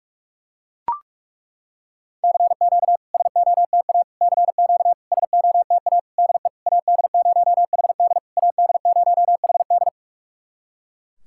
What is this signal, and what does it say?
Morse code, heard as a keyed tone, about 43 wpm, 700 Hz. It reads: CQ SOTA CQ SOTA DE AD0HD AD0HD
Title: SOTA QSO Practice 45wpm